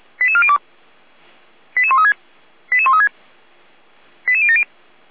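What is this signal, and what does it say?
Five-tone selective call (ZVEI), heard as quick runs of five tones, tones of about 70 ms. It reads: 88402 88136 88136 88070
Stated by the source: ZVEI five-tone call received by SDR.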